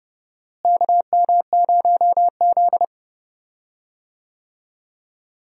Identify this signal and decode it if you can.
Morse code, heard as a keyed tone, about 30 wpm, 700 Hz. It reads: KM0Z
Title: USA Call Signs - 4 Characters - Repeated 3x - 30wpm